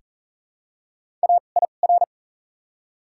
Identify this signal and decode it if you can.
Morse code, heard as a keyed tone, about 40 wpm, 700 Hz. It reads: A I R
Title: 2x Character Spacing - Top 300 Words - 40wpm